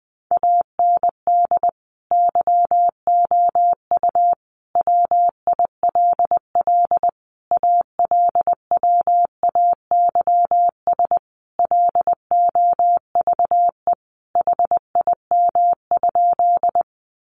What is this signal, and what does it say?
Morse code, heard as a keyed tone, about 20 wpm, 700 Hz. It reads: AND YOU WILL ALWAYS LOVE HIM?
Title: Sentences from Top 400 Words 20wpm